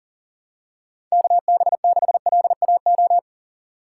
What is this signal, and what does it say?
Morse code, heard as a keyed tone, about 40 wpm, 700 Hz. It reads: KB6LAO
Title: USA Call Signs - 6 Characters - 2x Character Spacing - 40wpm